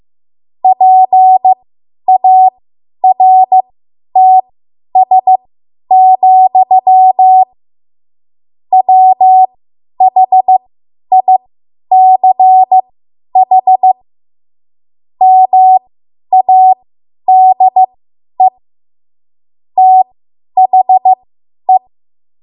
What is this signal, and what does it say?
Morse code, heard as a keyed tone, about 15 wpm, spaced as for 10 wpm, 750 Hz. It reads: PARTS, WHICH MADE THE